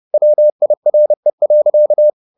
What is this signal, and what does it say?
Morse code, heard as a keyed tone, about 30 wpm, 600 Hz. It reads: WIRE.
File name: the old man and the sea / the old man and the sea 1 30wpm